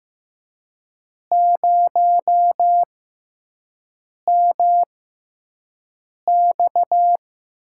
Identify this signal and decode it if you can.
Morse code, heard as a keyed tone, about 15 wpm, spaced as for 5 wpm, 700 Hz. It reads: 0MX
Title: USA Call Signs - 6x Character Spacing - 15wpm